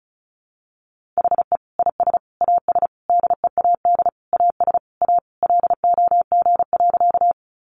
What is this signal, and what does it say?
Morse code, heard as a keyed tone, about 34 wpm, 700 Hz. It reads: HE IS AS DEAD AS A LOG.
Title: Sentences from Top 1500 Words 35wpm